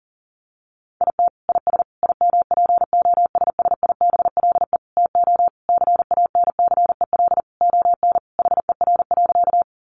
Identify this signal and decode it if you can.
Morse code, heard as a keyed tone, about 40 wpm, 700 Hz. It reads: IT IS IMPOSSIBLE TO CANCEL ON HER.
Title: Sentences from Top 1200 Words 40wpm